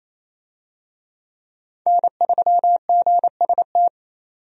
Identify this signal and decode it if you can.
Morse code, heard as a keyed tone, about 27 wpm, 700 Hz. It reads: N3GST
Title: USA Call Signs - 5 Characters - 28wpm